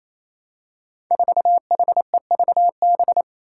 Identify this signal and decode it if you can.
Morse code, heard as a keyed tone, about 28 wpm, 700 Hz. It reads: 4HEVB
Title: Five Challenging Letter-Number Combinations 28wpm